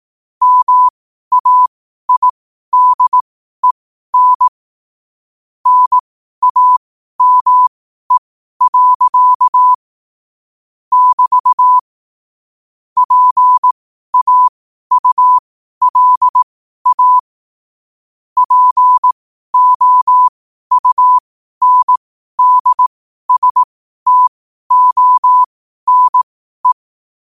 Morse code, heard as a keyed tone, about 18 wpm, spaced as for 12 wpm, 1000 Hz. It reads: MAIDEN NAME. <BT> PAULA POUNDSTONE
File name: qotdfiles / 12wpm